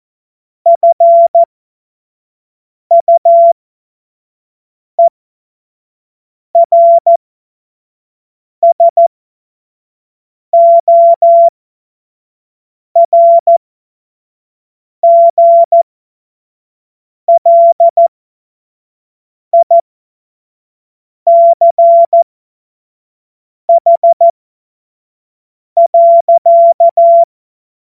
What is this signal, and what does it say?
Morse code, heard as a keyed tone, about 14 wpm, spaced as for 5 wpm, 680 Hz. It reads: FUERSORGLICH.
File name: Morsekurs / Witze / Witz 06 Februar 2024 25 F70